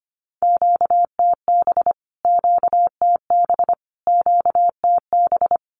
Morse code, heard as a keyed tone, about 25 wpm, 700 Hz. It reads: QTB QTB QTB
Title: Q-Codes Repeated 3x 25wpm